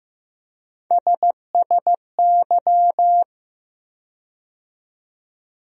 Morse code, heard as a keyed tone, about 15 wpm, 700 Hz. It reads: SSY
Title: USA Call Sign Postfixes - 3 Letters 15wpm